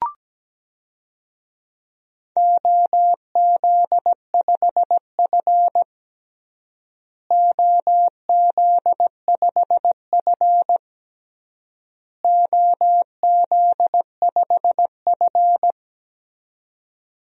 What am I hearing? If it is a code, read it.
Morse code, heard as a keyed tone, about 17 wpm, 700 Hz. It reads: OZ5F OZ5F OZ5F
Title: DXCC-100 Call Signs - 4 Characters - Repeated 3x - 17wpm